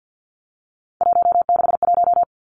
Morse code, heard as a keyed tone, about 50 wpm, 700 Hz. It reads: 161
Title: Three Challenging Letter-Number Combinations 50wpm